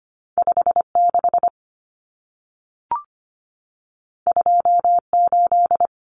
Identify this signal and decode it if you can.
Morse code, heard as a keyed tone, about 25 wpm, 700 Hz. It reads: 56 28
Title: Numbers - 2-Digit 25wpm